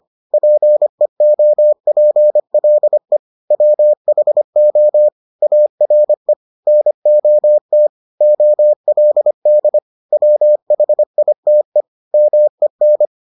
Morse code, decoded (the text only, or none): PEOPLE WHO ARE NOT OLD WHITE MEN